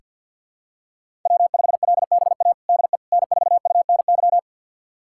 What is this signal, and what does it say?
Morse code, heard as a keyed tone, about 47 wpm, 700 Hz. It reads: W5LBA DE N4UNY
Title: USA Callsign DE Callsign - 5 Characters - 50wpm